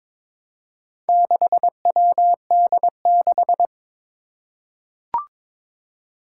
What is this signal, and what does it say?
Morse code, heard as a keyed tone, about 22 wpm, 700 Hz. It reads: 6WD6
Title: Four Challenging Letter-Number Combinations 22wpm